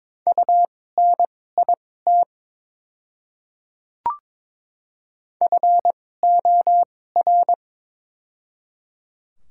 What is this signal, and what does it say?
Morse code, heard as a keyed tone, about 22 wpm, spaced as for 16 wpm, 700 Hz. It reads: UNIT FOR
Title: 2x Character Spacing - Top 500 Words - 22wpm